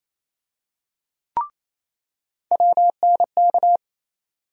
Morse code, heard as a keyed tone, about 28 wpm, 700 Hz. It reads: WNK
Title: USA Call Sign Postfixes - 3 Letters 28wpm